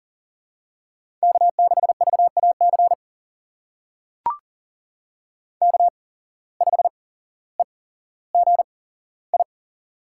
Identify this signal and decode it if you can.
Morse code, heard as a keyed tone, about 40 wpm, spaced as for 17 wpm, 700 Hz. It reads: K6VAC K5EGI
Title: USA Call Signs - 5 Characters - 8x Character Spacing - 40wpm